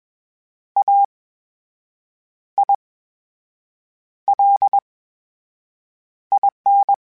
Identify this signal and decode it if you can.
Morse code, heard as a keyed tone, about 21 wpm, spaced as for 6 wpm, 800 Hz. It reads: AILIN
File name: Special INGs-mp3s / Special INGs-mp3s / VerySlowToFaster Ing @20